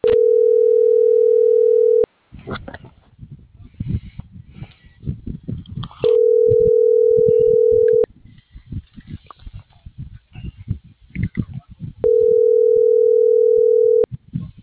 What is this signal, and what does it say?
Ringback tone (North America), 440 + 480 Hz.